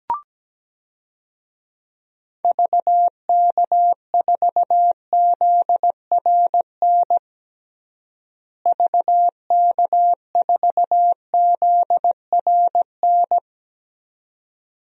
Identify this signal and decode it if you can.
Morse code, heard as a keyed tone, about 17 wpm, 700 Hz. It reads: VK4ZRN VK4ZRN
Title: DXCC-100 Call Signs - 6 Characters - Repeated 2x - 17wpm